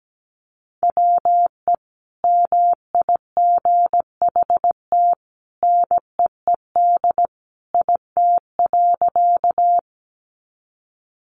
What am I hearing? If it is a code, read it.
Morse code, heard as a keyed tone, about 17 wpm, 700 Hz. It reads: WE MIGHT NEED IT.
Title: A Sound Of Thunder 17wpm